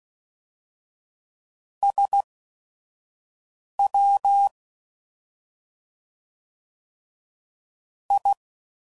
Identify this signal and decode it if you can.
Morse code, heard as a keyed tone, about 16 wpm, spaced as for 5 wpm, 800 Hz. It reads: SW I